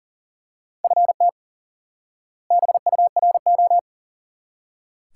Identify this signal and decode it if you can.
Morse code, heard as a keyed tone, about 37 wpm, 700 Hz. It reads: FT BURO
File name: course / instant-qso-element-course-3 / IQE Course 3 - Lesson 89 - YAGI - Two Elements - 40wpm